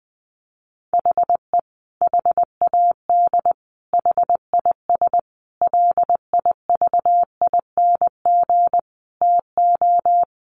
Morse code, heard as a keyed tone, about 20 wpm, 700 Hz. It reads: HE HAD HIS LIVING TO
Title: Sentences from Top 1500 Words 20wpm